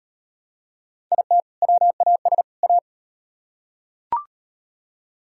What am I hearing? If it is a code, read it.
Morse code, heard as a keyed tone, about 36 wpm, 700 Hz. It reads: IT WAS A